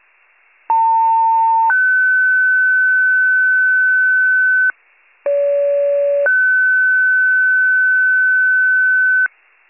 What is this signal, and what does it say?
Two-tone page, heard as a one-second tone, then a three-second tone, about 900 then 1500 Hz, about 570 then 1500 Hz. It is heard from 0.7 to 9.3 s.